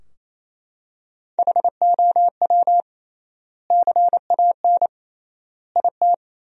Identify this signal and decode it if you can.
Morse code, heard as a keyed tone, about 28 wpm, 700 Hz. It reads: HOW CAN IT
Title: 3x Word-Spacing - Sentences from Top 100 Words - 28wpm